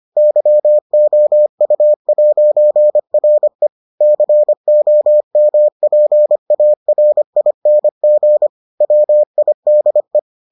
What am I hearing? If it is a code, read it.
Morse code, heard as a keyed tone, about 25 wpm, 600 Hz. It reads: YOU'RE COMPARING WIDE